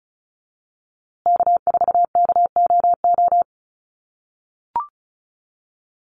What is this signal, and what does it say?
Morse code, heard as a keyed tone, about 34 wpm, 700 Hz. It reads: K4KOO
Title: USA Call Signs Repeated 2x 35wpm